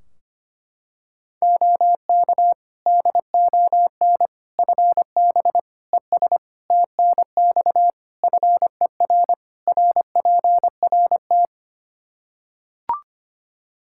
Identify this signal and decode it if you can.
Morse code, heard as a keyed tone, about 25 wpm, 700 Hz. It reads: OK DON FB ES TNX FER RPRT
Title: Practice QSO - Exchange 2 - Info - 25wpm